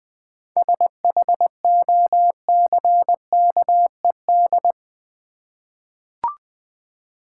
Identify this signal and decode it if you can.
Morse code, heard as a keyed tone, about 20 wpm, 690 Hz. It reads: SHOCKED
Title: 7 Letter Words - First 3 Letters Given - 20wpm